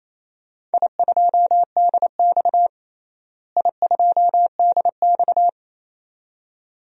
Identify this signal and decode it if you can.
Morse code, heard as a keyed tone, about 28 wpm, 700 Hz. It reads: I2DX I2DX
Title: DXCC-100 Call Signs - 4 Characters - Repeated 2x - 28wpm